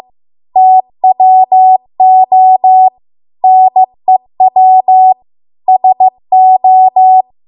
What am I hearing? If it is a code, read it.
Morse code, heard as a keyed tone, about 15 wpm, 750 Hz. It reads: TWO NEW SO
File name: Archive / 15 WPM / 15WPM